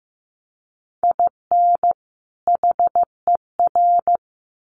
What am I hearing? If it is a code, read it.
Morse code, heard as a keyed tone, about 15 wpm, 700 Hz. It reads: IN HER
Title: Sets of 2 Words 15wpm